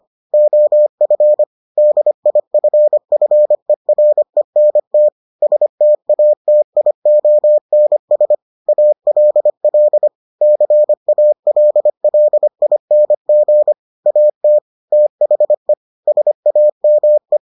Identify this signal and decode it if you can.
Morse code, heard as a keyed tone, about 25 wpm, 600 Hz. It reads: OF DIFFERENT STATIONS ALL CALLING AT THE SAME